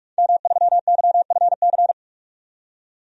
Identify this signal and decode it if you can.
Morse code, heard as a keyed tone, about 44 wpm, 700 Hz. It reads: M3YFC
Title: DXCC-100 Call Signs - 5 Characters - 45wpm